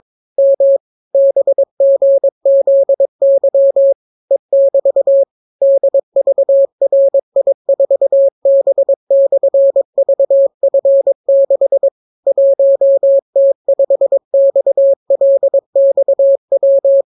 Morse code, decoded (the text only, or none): M BGZY E<BT> DVRI4B/VF6 1T5XLXW